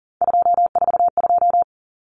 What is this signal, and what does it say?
Morse code, heard as a keyed tone, about 40 wpm, 700 Hz. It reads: 242
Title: Numbers - 3-Digit 40wpm